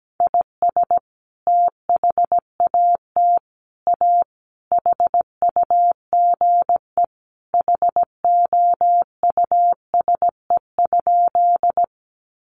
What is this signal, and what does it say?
Morse code, heard as a keyed tone, about 17 wpm, 700 Hz. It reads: IS THAT A HUGE HOUSE?
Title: Sentences from Top 1100 Words 17wpm